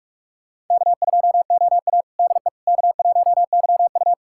Morse code, heard as a keyed tone, about 46 wpm, 700 Hz. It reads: K2OA DE K1YU
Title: USA Callsign DE Callsign - 4 Characters - 45wpm